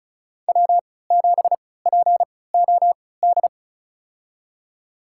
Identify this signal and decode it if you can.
Morse code, heard as a keyed tone, about 35 wpm, spaced as for 20 wpm, 700 Hz. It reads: W7POD
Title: USA Call Signs - 3x Character Spacing - 35wpm